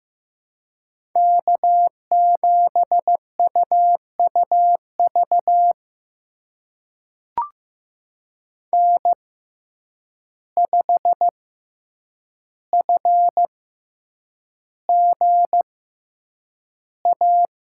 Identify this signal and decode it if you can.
Morse code, heard as a keyed tone, about 15 wpm, spaced as for 8 wpm, 700 Hz. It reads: K7UUV N5FGA
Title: USA Call Signs - 5 Characters - 6x Character Spacing - 15wpm